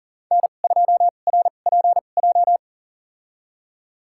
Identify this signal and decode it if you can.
Morse code, heard as a keyed tone, about 39 wpm, spaced as for 29 wpm, 700 Hz. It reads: N2RPJ
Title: USA Call Signs - 5 Characters - 2x Character Spacing - 40wpm